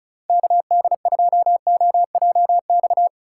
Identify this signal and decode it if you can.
Morse code, heard as a keyed tone, about 36 wpm, 700 Hz. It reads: KD2OJX
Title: USA Call Signs - 4x Character Spacing - 35wpm